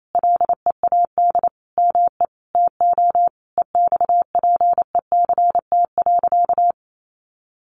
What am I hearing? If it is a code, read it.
Morse code, heard as a keyed tone, about 28 wpm, 700 Hz. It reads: LEAD ME TO EXPECT.
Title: Sentences from Top 1200 Words 28wpm